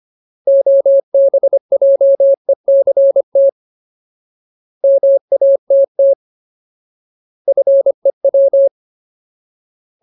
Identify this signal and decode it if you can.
Morse code, heard as a keyed tone, about 25 wpm, 550 Hz. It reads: OBJECT MATT FEW
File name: Ultimate Word List / 25WPM Ultimate